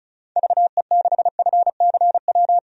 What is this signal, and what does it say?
Morse code, heard as a keyed tone, about 36 wpm, 700 Hz. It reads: VE6FCW
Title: DXCC-100 Call Signs - 6 Characters - 35wpm